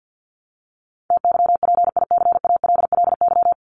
Morse code, heard as a keyed tone, about 50 wpm, 700 Hz. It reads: TYPICALLY
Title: LY Suffix - Top 100 Words with Base Words 50wpm